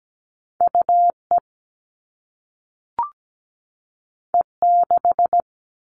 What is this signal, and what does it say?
Morse code, heard as a keyed tone, about 17 wpm, 700 Hz. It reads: UE E6
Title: All Cut Numbers - 2-Digit 17wpm